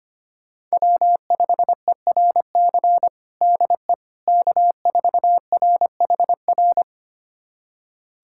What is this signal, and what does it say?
Morse code, heard as a keyed tone, about 25 wpm, 700 Hz. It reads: W5ERC DE K4RHR
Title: USA Callsign DE Callsign 25wpm